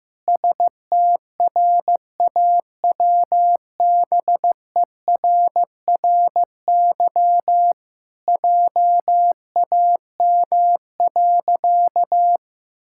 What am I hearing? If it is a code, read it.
Morse code, heard as a keyed tone, about 15 wpm, 700 Hz. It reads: STRAWBERRY JAM.
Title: Pangram Challenge v2 - 15wpm